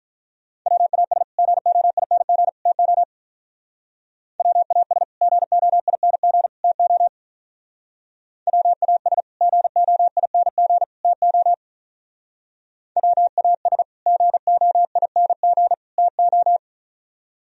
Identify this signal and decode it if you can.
Morse code, heard as a keyed tone, about 43 wpm, 700 Hz. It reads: WAS GOING TO WAS GOING TO WAS GOING TO WAS GOING TO
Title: Speed-Racing - Sets of 3 Words 35wpm